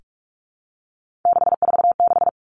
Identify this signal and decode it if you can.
Morse code, heard as a keyed tone, about 46 wpm, 700 Hz. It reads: 646
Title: Three Challenging Letter-Number Combinations 45wpm